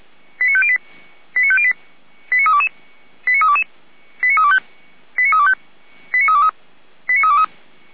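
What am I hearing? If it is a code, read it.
Five-tone selective call (ZVEI), heard as quick runs of five tones, tones of about 70 ms. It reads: 89508 89508 89322 89322 89325 89325 89323 89323
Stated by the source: Five-tone ZVEI selective call received on an SDR receiver, recorded by the fire brigade Feuerwehr Orlamünde.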